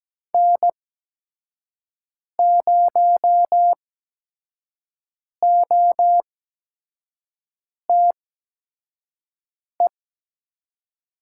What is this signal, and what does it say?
Morse code, heard as a keyed tone, about 17 wpm, spaced as for 5 wpm, 700 Hz. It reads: N0OTE
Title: USA Call Signs - 8x Character Spacing - 17wpm